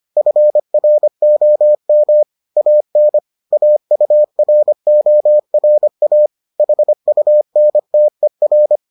Morse code, heard as a keyed tone, about 25 wpm, 600 Hz. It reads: FROM AN AURORA HUNTER